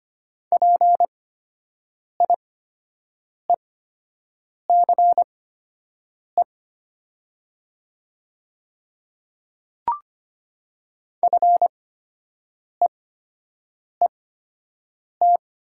Morse code, heard as a keyed tone, about 25 wpm, spaced as for 7 wpm, 700 Hz. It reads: PIECE FEET
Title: 8x Character Spacing - Top 500 Words - 25wpm